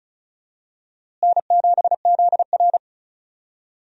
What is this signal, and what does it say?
Morse code, heard as a keyed tone, about 36 wpm, 700 Hz. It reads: N7ZR
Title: USA Call Signs - 4 Characters - Repeated 2x - 35wpm